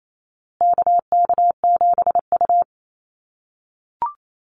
Morse code, heard as a keyed tone, about 28 wpm, 700 Hz. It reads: KK7U
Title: USA Call Signs Repeated 2x 28wpm